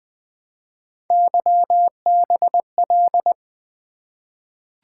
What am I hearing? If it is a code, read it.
Morse code, heard as a keyed tone, about 20 wpm, 700 Hz. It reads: YBL
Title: Three Letter-Number Combinations - 20wpm